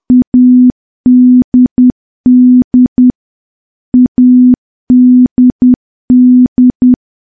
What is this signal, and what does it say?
Morse code, heard as a keyed tone, about 10 wpm, 250 Hz. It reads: ADD ADD
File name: morse / morse-training-10WPM